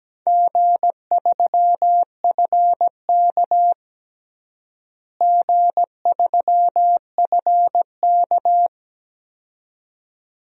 Morse code, heard as a keyed tone, about 17 wpm, 700 Hz. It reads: G3FK G3FK
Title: DXCC-100 Call Signs - 4 Characters - Repeated 2x - 17wpm